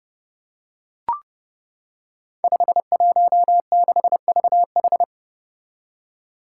Morse code, heard as a keyed tone, about 30 wpm, 700 Hz. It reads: 516VH
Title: Five Challenging Letter-Number Combinations 30wpm